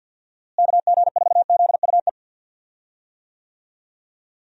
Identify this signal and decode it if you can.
Morse code, heard as a keyed tone, about 50 wpm, 700 Hz. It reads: KG4ZUE
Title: USA Call Signs - 6 Characters - Repeated 2x - 50wpm